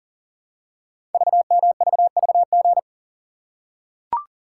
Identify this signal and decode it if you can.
Morse code, heard as a keyed tone, about 43 wpm, 700 Hz. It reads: VMVVG
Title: Code Groups - 5 Letters - 40wpm